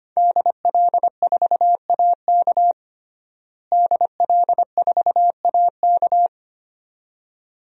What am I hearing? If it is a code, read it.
Morse code, heard as a keyed tone, about 25 wpm, 700 Hz. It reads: DL4AK DL4AK
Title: DXCC-100 Call Signs - 5 Characters - Repeated 2x - 25wpm